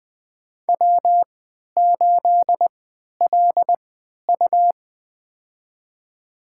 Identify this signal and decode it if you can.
Morse code, heard as a keyed tone, about 20 wpm, spaced as for 11 wpm, 700 Hz. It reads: W8LU
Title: USA Call Signs - 4 Characters - 3x Character Spacing - 20wpm